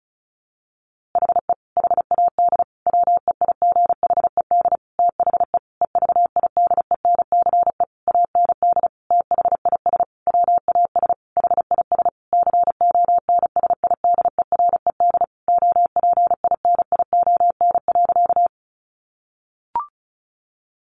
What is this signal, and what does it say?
Morse code, heard as a keyed tone, about 34 wpm, 700 Hz. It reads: HE HAD WEIGHED THE EVIDENCE AND THIS WAS HIS CONSIDERED OPINION.
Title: A Sound Of Thunder 35wpm